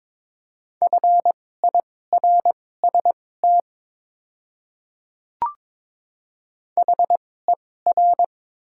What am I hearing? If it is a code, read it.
Morse code, heard as a keyed tone, about 22 wpm, spaced as for 16 wpm, 700 Hz. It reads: FIRST HER
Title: 2x Character Spacing - Top 100 Words - 22wpm